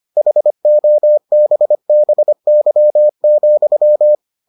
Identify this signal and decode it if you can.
Morse code, heard as a keyed tone, about 25 wpm, 600 Hz. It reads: HOBBY,